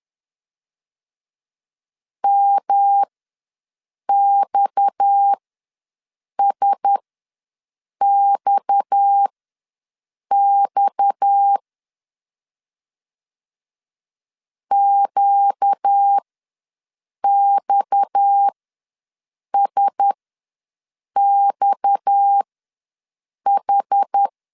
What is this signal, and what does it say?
Morse code, heard as a keyed tone, about 10 wpm, spaced as for 6 wpm, 800 Hz. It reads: MXSXX QXSXH
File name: LEZIONI / 6 PKX0 / 6 PKX ws 10wpm 3706